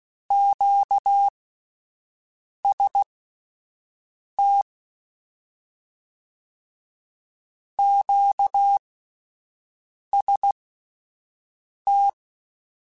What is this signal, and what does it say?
Morse code, heard as a keyed tone, about 16 wpm, spaced as for 5 wpm, 800 Hz. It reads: QST QST
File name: Combinations / Combinations-mp3s / Combinations@5